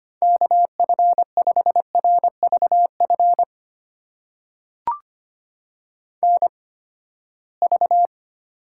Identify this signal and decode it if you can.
Morse code, heard as a keyed tone, about 25 wpm, spaced as for 17 wpm, 700 Hz. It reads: KF5RVF NV
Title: USA Call Signs - 8x Character Spacing - 25wpm